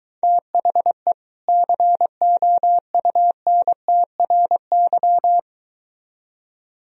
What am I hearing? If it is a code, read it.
Morse code, heard as a keyed tone, about 23 wpm, 700 Hz. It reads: THE COUNTRY